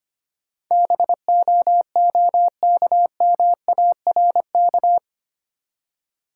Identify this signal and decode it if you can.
Morse code, heard as a keyed tone, about 25 wpm, 690 Hz. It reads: BOOKMARK
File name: practice / 25wpm / Easy Words with Hints - 25wpm